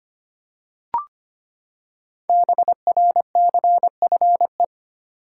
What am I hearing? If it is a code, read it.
Morse code, heard as a keyed tone, about 25 wpm, 700 Hz. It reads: BRCFE
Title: Code Groups - 5 Letters - 25wpm